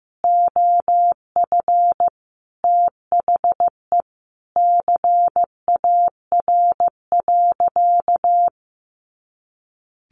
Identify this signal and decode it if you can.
Morse code, heard as a keyed tone, about 15 wpm, 700 Hz. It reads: OF THE CAR.
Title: Easy Phrases 15wpm